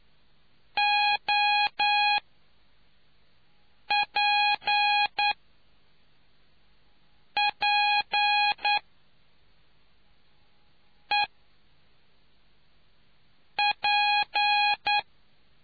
Morse code, heard as a keyed tone, about 9 wpm, spaced as for 3 wpm, 830 Hz. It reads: OPPEP